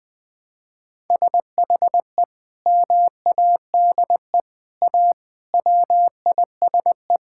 Morse code, heard as a keyed tone, about 21 wpm, 700 Hz. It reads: SHE MADE A WISE
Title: Sentences from Top 1600 Words 20wpm